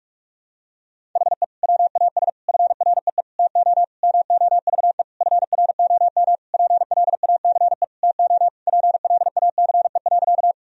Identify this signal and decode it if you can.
Morse code, heard as a keyed tone, about 46 wpm, 700 Hz. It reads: HE WAS FREE TO MOVE FROM PLACE TO PLACE.
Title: Sentences from Top 1000 Words - Review Part 1 - 45wpm